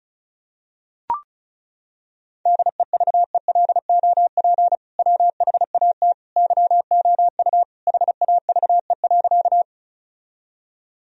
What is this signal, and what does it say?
Morse code, heard as a keyed tone, about 34 wpm, 700 Hz. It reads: DEVELOP WHAT YOU HAVE.
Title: Sentences from Top 1000 Words - Review Part 2 - 35wpm